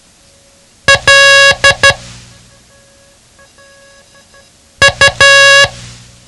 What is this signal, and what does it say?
Morse code, heard as a keyed tone, about 12 wpm, spaced as for 3 wpm, 1650 Hz. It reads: LU
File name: ham radio / CodeQuick / Code Quick 07